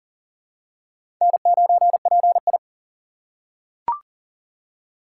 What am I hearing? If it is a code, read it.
Morse code, heard as a keyed tone, about 40 wpm, 700 Hz. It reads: N9PI
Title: USA Call Signs - 4 Characters - 6x Character Spacing - 40wpm